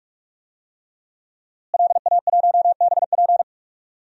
Morse code, heard as a keyed tone, about 46 wpm, 700 Hz. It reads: LA1BP
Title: DXCC-100 Call Signs - 5 Characters - Repeated 3x - 45wpm